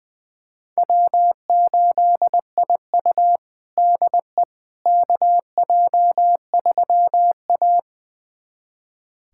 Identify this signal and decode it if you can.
Morse code, heard as a keyed tone, about 20 wpm, 700 Hz. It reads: W8IU DE KJ3A
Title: USA Callsign DE Callsign - 4 Characters - 20wpm